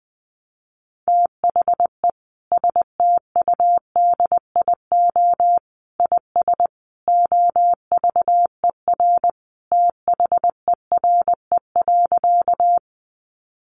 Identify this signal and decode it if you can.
Morse code, heard as a keyed tone, about 20 wpm, 700 Hz. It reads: THE STUDIO IS OVER THERE.
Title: Sentences from Top 1200 Words 20wpm